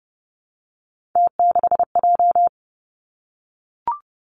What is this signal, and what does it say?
Morse code, heard as a keyed tone, about 30 wpm, 700 Hz. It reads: T6J
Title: Three Challenging Letter-Number Combinations 30wpm